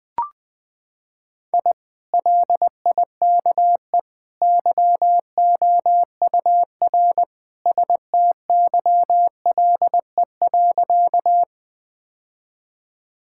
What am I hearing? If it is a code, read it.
Morse code, heard as a keyed tone, about 21 wpm, 700 Hz. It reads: I LIKE YOUR STYLE.
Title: Sentences from Top 1000 Words - Review Part 5 - 20wpm